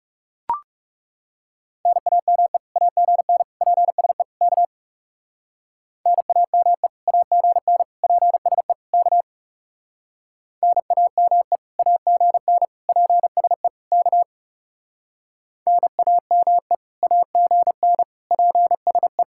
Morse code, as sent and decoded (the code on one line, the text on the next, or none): -. .- -- . / .- --. -. / .--. ... / -.- / -. .- -- . / .- --. -. / .--. ... / -.- / -. .- -- . / .- --. -. / .--. ... / -.- / -. .- -- . / .- --. -. / .--. ... .
NAME AGN PSE K NAME AGN PSE K NAME AGN PSE K NAME AGN PSE